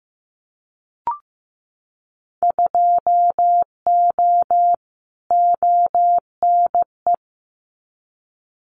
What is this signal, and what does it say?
Morse code, heard as a keyed tone, about 15 wpm, 700 Hz. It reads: 2O ONE
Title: Winter Field Day Exchanges Encore 15wpm